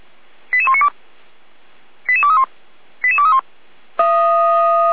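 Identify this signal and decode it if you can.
Five-tone selective call (ZVEI), heard as quick runs of five tones, tones of about 70 ms. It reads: 88192 88231 88231
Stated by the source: Five-tone ZVEI selective call received by SDR.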